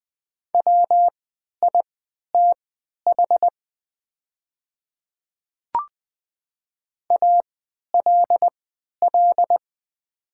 Morse code, heard as a keyed tone, about 20 wpm, spaced as for 11 wpm, 700 Hz. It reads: WITH ALL
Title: 3x Character Spacing - Top 100 Words - 20wpm